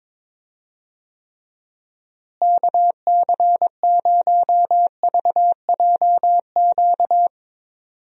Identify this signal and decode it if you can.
Morse code, heard as a keyed tone, about 21 wpm, 700 Hz. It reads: KC0VJQ
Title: USA Call Signs - 6 Characters - Repeated 3x - 22wpm